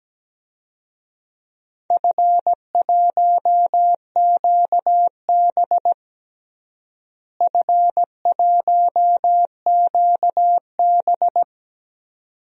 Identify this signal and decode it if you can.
Morse code, heard as a keyed tone, about 17 wpm, 700 Hz. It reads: F1QB F1QB
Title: DXCC-100 Call Signs - 4 Characters - Repeated 2x - 17wpm